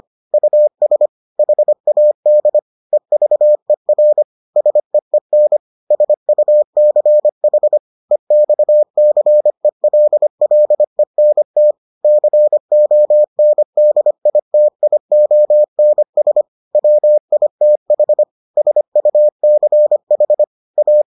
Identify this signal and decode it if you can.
Morse code, heard as a keyed tone, about 25 wpm, 600 Hz. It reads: US HAD EVER SEEN SUCH EXCELLENT CONDITIONS WITH SUCH A